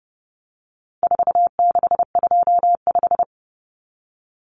Morse code, heard as a keyed tone, about 30 wpm, 700 Hz. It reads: 4625
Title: Common Cut Numbers - 4-Digit 30wpm